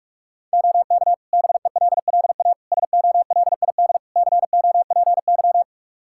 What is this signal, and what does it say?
Morse code, heard as a keyed tone, about 46 wpm, 700 Hz. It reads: OK BELLA SOLID COPY
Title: K1USN SST QSO Practice 45wpm